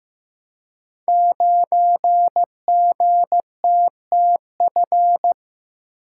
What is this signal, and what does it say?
Morse code, heard as a keyed tone, about 15 wpm, 700 Hz. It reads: 9GTTF
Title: Code Groups - 5 Letter:Numbers - 15wpm